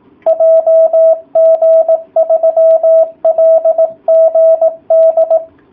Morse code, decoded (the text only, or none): JG3LGD